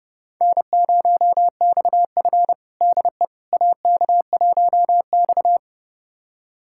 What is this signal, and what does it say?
Morse code, heard as a keyed tone, about 31 wpm, 700 Hz. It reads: N0XF DE AK1X
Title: USA Callsign DE Callsign - 4 Characters - 30wpm